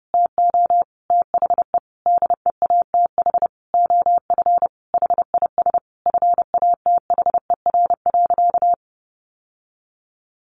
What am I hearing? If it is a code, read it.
Morse code, heard as a keyed tone, about 30 wpm, 700 Hz. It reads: TO THE DEATH OF HIS FATHER.
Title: Sentences from Top 1500 Words 30wpm